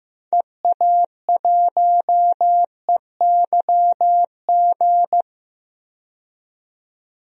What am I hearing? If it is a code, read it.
Morse code, heard as a keyed tone, about 15 wpm, 700 Hz. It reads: EA1EYG
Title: DXCC-100 Call Signs - 6 Characters - 15wpm